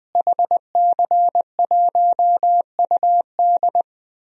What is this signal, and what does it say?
Morse code, heard as a keyed tone, about 21 wpm, 700 Hz. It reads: HC1UD
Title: DXCC-100 Call Signs - 5 Characters - Repeated 3x - 20wpm